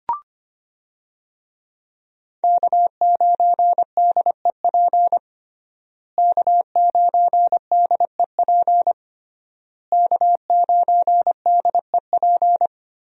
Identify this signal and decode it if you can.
Morse code, heard as a keyed tone, about 25 wpm, 700 Hz. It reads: K9DEP K9DEP K9DEP
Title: USA Call Signs - 5 Characters - Repeated 3x - 25wpm